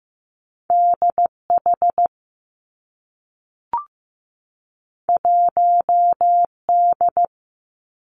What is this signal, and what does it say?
Morse code, heard as a keyed tone, about 15 wpm, 700 Hz. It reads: D H 1 D
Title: Two Challenging Letter-Number Combinations 15wpm